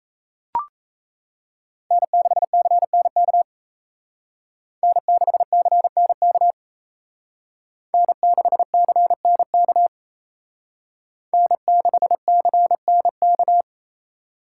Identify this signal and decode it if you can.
Morse code, heard as a keyed tone, about 36 wpm, 700 Hz. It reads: N6CNK N6CNK N6CNK N6CNK